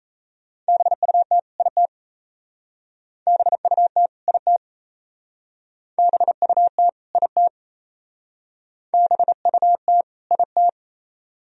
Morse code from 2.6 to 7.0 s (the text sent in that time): BUT IT BUT